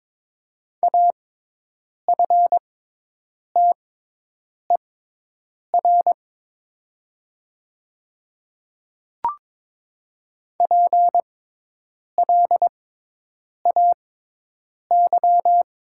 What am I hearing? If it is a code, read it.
Morse code, heard as a keyed tone, about 22 wpm, spaced as for 8 wpm, 700 Hz. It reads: AFTER PLAY
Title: 6x Character Spacing - Top 300 Words - 22wpm